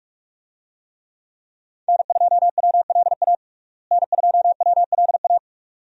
Morse code, heard as a keyed tone, about 46 wpm, 700 Hz. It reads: N2WLA N2WLA
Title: USA Call Signs - 5 Characters - Repeated 2x - 45wpm